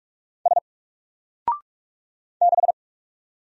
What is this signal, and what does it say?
Morse code, heard as a keyed tone, about 46 wpm, 700 Hz. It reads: S 6
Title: Single Character (Letter:Number:Punctuation:Prosign) Alt 45wpm